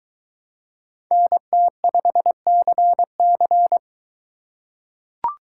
Morse code, heard as a keyed tone, about 20 wpm, 700 Hz. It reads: NT5CC